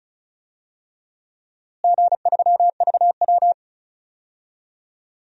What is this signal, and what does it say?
Morse code, heard as a keyed tone, about 36 wpm, 700 Hz. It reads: G3VW
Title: DXCC-100 Call Signs - 4 Characters - Repeated 3x - 35wpm